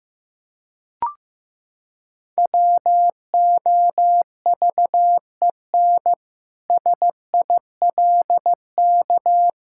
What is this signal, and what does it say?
Morse code, heard as a keyed tone, about 15 wpm, 700 Hz. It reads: WOVEN SILK
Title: Pangram Challenge v2 - 15wpm